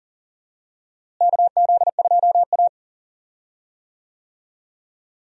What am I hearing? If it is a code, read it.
Morse code, heard as a keyed tone, about 40 wpm, 700 Hz. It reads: KZ2A
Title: USA Call Signs - 4 Characters - Repeated 3x - 40wpm